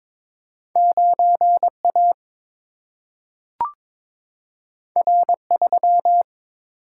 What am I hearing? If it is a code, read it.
Morse code, heard as a keyed tone, about 22 wpm, 700 Hz. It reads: 9 A R 3
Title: Single-Letter Single-Number 22wpm